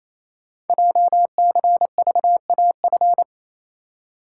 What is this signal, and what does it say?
Morse code, heard as a keyed tone, about 29 wpm, 690 Hz. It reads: JCVAF
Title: Code Groups - 5 Letters - 28wpm